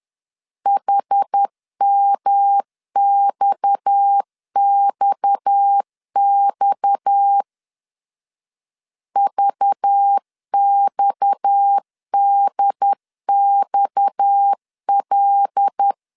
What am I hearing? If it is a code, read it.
Morse code, heard as a keyed tone, about 10 wpm, 800 Hz. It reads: HMXXX VXDXL
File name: LEZIONI / 6 PKX0 / 6 PKX 10wpm 2426